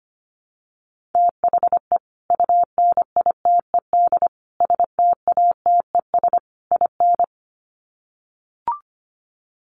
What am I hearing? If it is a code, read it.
Morse code, heard as a keyed tone, about 25 wpm, 700 Hz. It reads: THE UNITED STATES IN
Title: Sets of 4 Words 25wpm